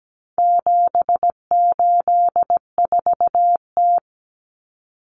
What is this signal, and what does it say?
Morse code, heard as a keyed tone, about 17 wpm, 700 Hz. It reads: 784T
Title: Common Cut Numbers - 4-Digit 17wpm